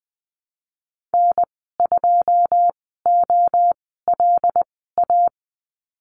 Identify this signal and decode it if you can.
Morse code, heard as a keyed tone, about 20 wpm, spaced as for 14 wpm, 700 Hz. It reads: N2OLA
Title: 088 Lesson 15g - Number 5 - Call Signs 20wpm